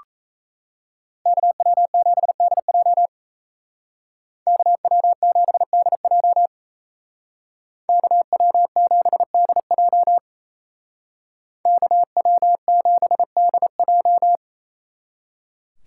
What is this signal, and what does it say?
Morse code, heard as a keyed tone, about 33 wpm, 700 Hz. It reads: KW7DJ KW7DJ KW7DJ KW7DJ
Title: Speed Racing - USA 2x2 Call Signs 28wpm